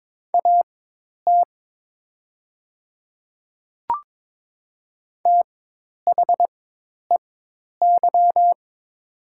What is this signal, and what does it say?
Morse code, heard as a keyed tone, about 22 wpm, spaced as for 10 wpm, 700 Hz. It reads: AT THEY
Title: 4x Character Spacing - Top 100 Words - 22wpm